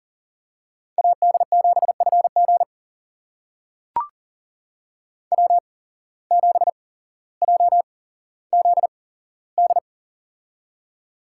Morse code, heard as a keyed tone, about 40 wpm, spaced as for 17 wpm, 700 Hz. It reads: AD7FG W7JZD
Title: USA Call Signs - 5 Characters - 8x Character Spacing - 40wpm